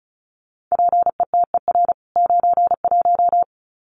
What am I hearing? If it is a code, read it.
Morse code, heard as a keyed tone, about 34 wpm, 700 Hz. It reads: PETER 91
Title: CWT Exchange 35wpm